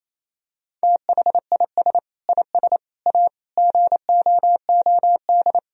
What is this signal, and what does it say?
Morse code, heard as a keyed tone, about 28 wpm, 700 Hz. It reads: THIS IS A GOOD
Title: Sentences from Top 1000 Words - Review Part 3 - 28wpm